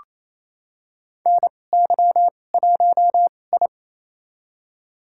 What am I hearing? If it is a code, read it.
Morse code, heard as a keyed tone, about 28 wpm, spaced as for 20 wpm, 700 Hz. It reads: NY1I
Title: USA Call Signs - 2x Character Spacing - 28wpm